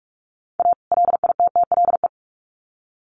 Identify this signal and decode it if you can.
Morse code, heard as a keyed tone, about 44 wpm, 700 Hz. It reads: A LITTLE
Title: Sets of 2 Words 45wpm